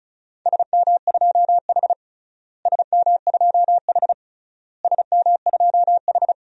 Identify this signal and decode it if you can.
Morse code, heard as a keyed tone, about 36 wpm, 700 Hz. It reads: SM2H SM2H SM2H
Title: DXCC-100 Call Signs - 4 Characters - Repeated 3x - 35wpm